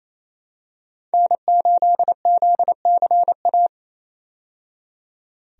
Morse code, heard as a keyed tone, about 28 wpm, 700 Hz. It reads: N8ZCA
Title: USA Call Signs - 5 Characters - 28wpm